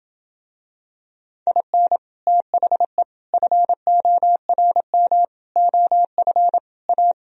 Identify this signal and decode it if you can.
Morse code, heard as a keyed tone, about 27 wpm, 700 Hz. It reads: IN THE FORM OF A